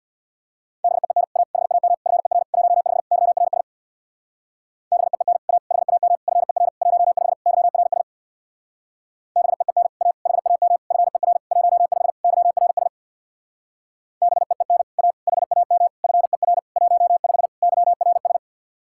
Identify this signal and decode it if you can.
Morse code, heard as a keyed tone, about 62 wpm, 700 Hz. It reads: BEEN A HAM FER 15 YRS BEEN A HAM FER 15 YRS BEEN A HAM FER 15 YRS BEEN A HAM FER 15 YRS
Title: Speed-Racing Common QSO Phrases - 50wpm